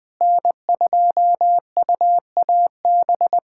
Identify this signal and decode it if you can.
Morse code, heard as a keyed tone, about 20 wpm, 700 Hz. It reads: N2UAB
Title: USA Callsign DE Callsign - 5 Characters - 20wpm